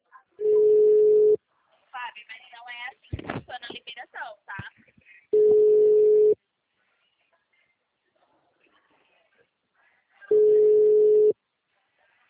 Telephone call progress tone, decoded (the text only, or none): ringback tone